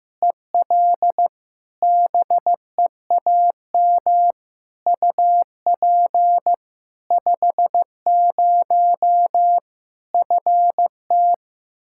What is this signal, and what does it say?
Morse code, heard as a keyed tone, about 15 wpm, 700 Hz. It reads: EL BEAM UP 50 FT
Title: Practice QSO - Exchange 2 - Info - 15wpm